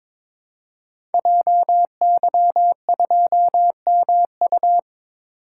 Morse code, heard as a keyed tone, about 22 wpm, 700 Hz. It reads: JY2MU
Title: Code Groups - 5 Letter:Numbers - 22wpm